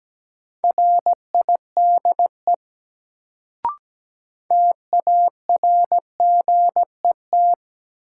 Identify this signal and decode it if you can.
Morse code, heard as a keyed tone, about 17 wpm, 700 Hz. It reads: RIDE TARGET
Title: Sentences from Top 1200 Words - New Word Drill Alt - 17wpm